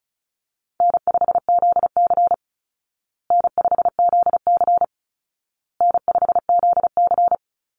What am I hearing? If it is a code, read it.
Morse code, heard as a keyed tone, about 34 wpm, 700 Hz. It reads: N5ZC N5ZC N5ZC
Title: Easy Call Signs Repeated 3x - 35wpm